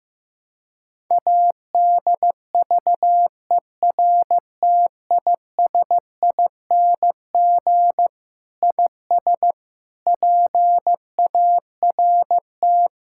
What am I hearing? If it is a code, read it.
Morse code, heard as a keyed tone, about 15 wpm, 700 Hz. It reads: ADVERTISING IS PART
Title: Sentences from Top 1000 Words - Review Part 6 - 15wpm